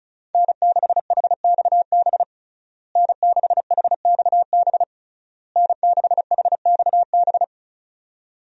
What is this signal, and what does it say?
Morse code, heard as a keyed tone, about 36 wpm, 700 Hz. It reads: N6HXB N6HXB N6HXB
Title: USA Call Signs - 5 Characters - Repeated 3x - 35wpm